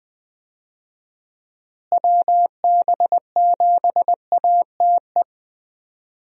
Morse code, heard as a keyed tone, about 20 wpm, 700 Hz. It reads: WB7ATE
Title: USA Call Signs - 6 Characters - Repeated 2x - 20wpm